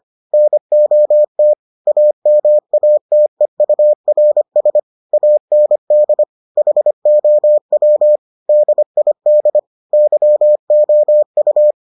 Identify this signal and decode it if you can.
Morse code, heard as a keyed tone, about 25 wpm, 600 Hz. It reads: NOT AMATEURS AND HOW DID YOU